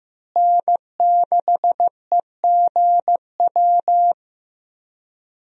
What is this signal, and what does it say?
Morse code, heard as a keyed tone, about 15 wpm, 700 Hz. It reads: N6EGW
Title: USA Call Signs - 5 Characters - Repeated 3x - 15wpm